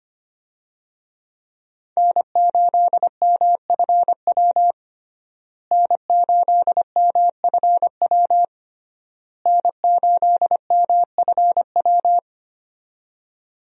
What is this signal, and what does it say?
Morse code, heard as a keyed tone, about 25 wpm, 700 Hz. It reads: N8MFW N8MFW N8MFW
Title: USA Call Signs - 5 Characters - Repeated 3x - 25wpm